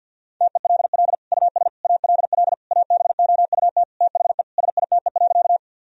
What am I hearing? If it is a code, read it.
Morse code, heard as a keyed tone, about 52 wpm, 700 Hz. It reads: TELL US ALL ABOUT THE SITE.
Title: Sentences from Top 1000 Words - Review Part 4 - 50wpm